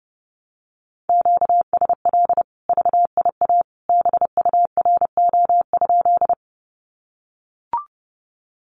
Morse code, heard as a keyed tone, about 30 wpm, 700 Hz. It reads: QSL VIA BURO?
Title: Common QSO Phrases 30wpm